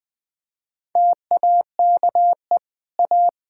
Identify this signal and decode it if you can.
Morse code, heard as a keyed tone, about 20 wpm, 700 Hz. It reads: TAKE A
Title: Build A Phrase - 5-Word Phrases - 20wpm